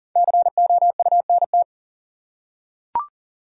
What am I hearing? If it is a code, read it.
Morse code, heard as a keyed tone, about 40 wpm, 700 Hz. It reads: COUNT
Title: Five Letter Words - 40wpm